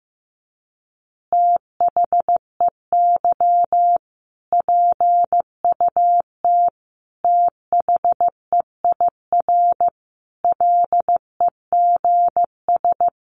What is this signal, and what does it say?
Morse code, heard as a keyed tone, about 15 wpm, 700 Hz. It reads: THEY PUT THEIR LEGS
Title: Easy Phrases 15wpm